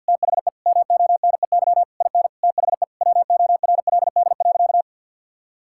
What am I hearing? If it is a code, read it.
Morse code, heard as a keyed tone, about 50 wpm, 700 Hz. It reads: THE MONEY IN THE WORLD.
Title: Sentences from Top 1000 Words - Review Part 6 - 50wpm